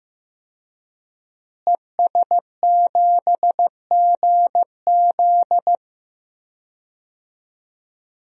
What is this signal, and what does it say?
Morse code, heard as a keyed tone, about 15 wpm, 700 Hz. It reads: ES7GZ
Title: DXCC-100 Call Signs - 5 Characters - Repeated 2x - 15wpm